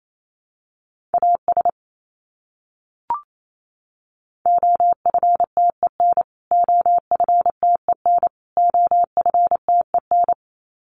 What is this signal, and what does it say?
Morse code, heard as a keyed tone, about 28 wpm, 700 Hz. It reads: AS OFTEN OFTEN OFTEN
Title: Top 500 Words Repeated 28wpm